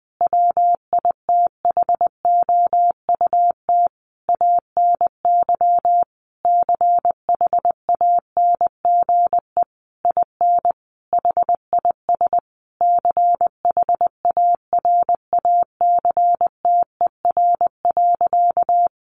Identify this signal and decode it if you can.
Morse code, heard as a keyed tone, about 19 wpm, 700 Hz. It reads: WITHOUT ANY CHANGE IN HIS CHARACTER.
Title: Sentences from Top 1200 Words 20wpm